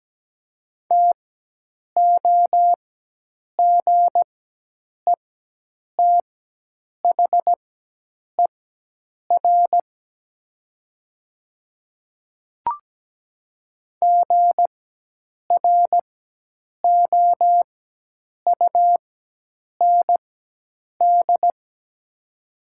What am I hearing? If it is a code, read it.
Morse code, heard as a keyed tone, about 17 wpm, spaced as for 8 wpm, 700 Hz. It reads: TOGETHER GROUND
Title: 4x Character Spacing - Top 500 Words - 17wpm